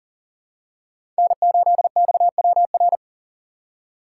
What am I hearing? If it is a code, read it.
Morse code, heard as a keyed tone, about 40 wpm, 700 Hz. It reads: N8XWR
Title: USA Call Signs - 5 Characters - 3x Character Spacing - 40wpm